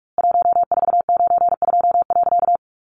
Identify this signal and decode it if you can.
Morse code, heard as a keyed tone, about 44 wpm, 700 Hz. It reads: 1492.
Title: A Sound Of Thunder 45wpm